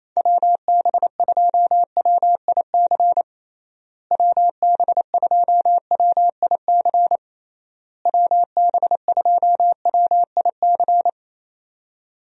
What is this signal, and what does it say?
Morse code, heard as a keyed tone, about 28 wpm, 700 Hz. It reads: WB2WIC WB2WIC WB2WIC
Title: USA Call Signs - 6 Characters - Repeated 3x - 28wpm